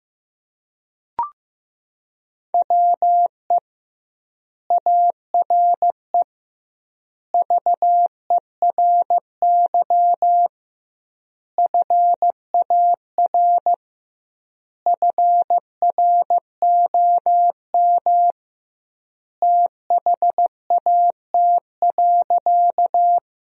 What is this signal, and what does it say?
Morse code, heard as a keyed tone, about 15 wpm, 700 Hz. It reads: WE ARE VERY FAR FROM THAT.
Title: 2x Word-Spacing - Sentences from Top 300 Words - 15wpm